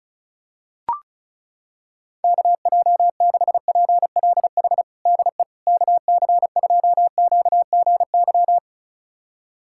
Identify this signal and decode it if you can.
Morse code, heard as a keyed tone, about 36 wpm, 700 Hz. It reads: KJ6PLH DE KC2QGY
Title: USA Callsign DE Callsign - 6 Characters - 35wpm